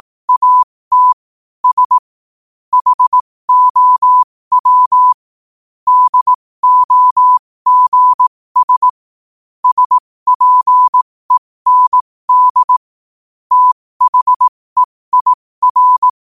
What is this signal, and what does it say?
Morse code, heard as a keyed tone, about 18 wpm, spaced as for 15 wpm, 1000 Hz. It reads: AT S HOW DOGS SPEND THEIR
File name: qotdfiles / 15wpm